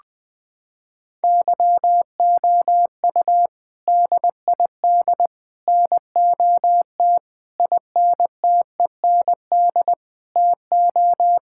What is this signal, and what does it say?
Morse code, heard as a keyed tone, about 20 wpm, 700 Hz. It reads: YOU DID NOT INTEND TO
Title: Sentences from Top 1600 Words 20wpm